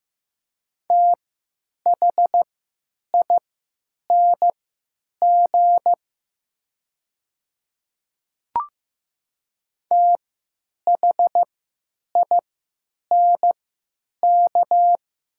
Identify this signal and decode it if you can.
Morse code, heard as a keyed tone, about 15 wpm, spaced as for 9 wpm, 700 Hz. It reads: THING THINK
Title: 3x Character Spacing - Top 500 Words - 15wpm